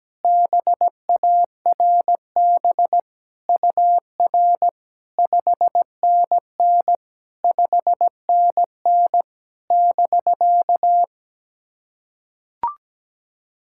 Morse code, heard as a keyed tone, about 17 wpm, 700 Hz. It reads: BARB UR 5NN 5NN <BK>
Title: SOTA QSO Practice 17wpm